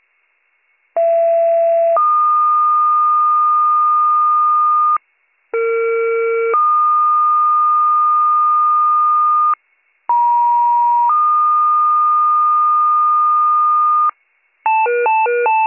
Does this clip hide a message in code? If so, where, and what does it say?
Yes: two-tone page from 1.0 to 14.1 s, heard as a one-second tone, then a three-second tone, about 670 then 1220 Hz, about 460 then 1220 Hz, about 950 then 1220 Hz.